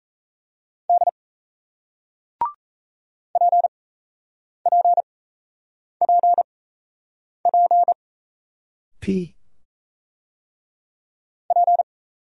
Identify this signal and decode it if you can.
Morse code, heard as a keyed tone, about 38 wpm, spaced as for 8 wpm, 700 Hz. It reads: D PPPP P